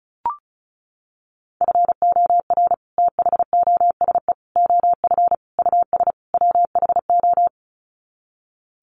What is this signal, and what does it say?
Morse code, heard as a keyed tone, about 34 wpm, 700 Hz. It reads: FOR THOSE OF US WHO
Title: Sets of 5 Words Encore 35wpm